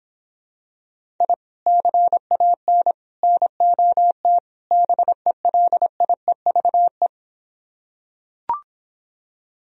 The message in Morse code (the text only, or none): I CAN NOT BELIEVE